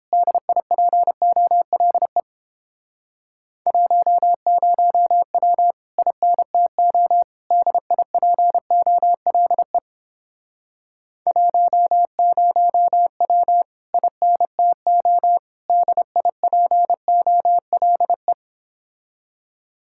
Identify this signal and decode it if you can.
Morse code, heard as a keyed tone, about 29 wpm, 700 Hz. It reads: DIPOLE 10W INTO DIPOLE 10W INTO DIPOLE